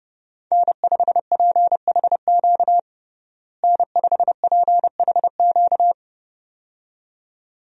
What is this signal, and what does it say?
Morse code, heard as a keyed tone, about 30 wpm, 700 Hz. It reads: N5PHQ N5PHQ
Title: USA Call Signs - 5 Characters - Repeated 2x - 30wpm